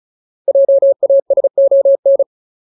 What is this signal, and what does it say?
Morse code, heard as a keyed tone, about 34 wpm, 550 Hz. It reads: JASON
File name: Ultimate Word List / 35WPM Ultimate